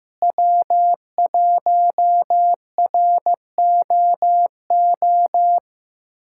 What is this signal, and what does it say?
Morse code, heard as a keyed tone, about 15 wpm, 700 Hz. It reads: W1ROO
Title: USA Callsign DE Callsign - 5 Characters - 15wpm